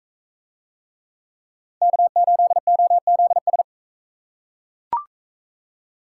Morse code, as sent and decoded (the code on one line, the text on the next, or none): -.- ---.. --- --.. ...
K8OZS